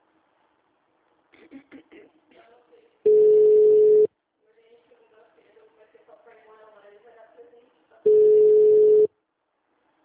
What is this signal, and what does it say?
Ringback tone (North America), 440 + 480 Hz, with speech.
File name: out- -2008- - - .1069